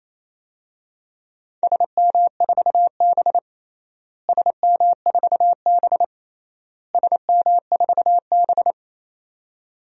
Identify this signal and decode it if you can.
Morse code, heard as a keyed tone, about 28 wpm, 700 Hz. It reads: SM4B SM4B SM4B
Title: DXCC-100 Call Signs - 4 Characters - Repeated 3x - 28wpm